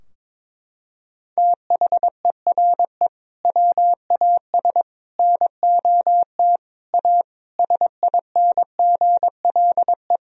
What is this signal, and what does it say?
Morse code, heard as a keyed tone, about 22 wpm, 700 Hz. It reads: THERE WAS NOT A SINGLE
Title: Sentences from Top 1000 Words - Review Part 4 - 22wpm